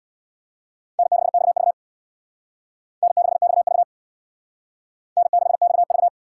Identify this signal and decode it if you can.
Morse code, heard as a keyed tone, about 62 wpm, 700 Hz. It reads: N6XV N6XV N6XV
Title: Speed Racing - USA 1x2 Call Signs 50wpm